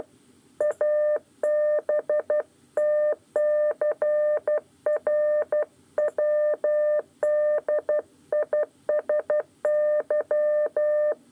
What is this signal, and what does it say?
Morse code, heard as a keyed tone, about 12 wpm, 600 Hz. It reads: ABTCR WDISY
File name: 8 lezione cw 1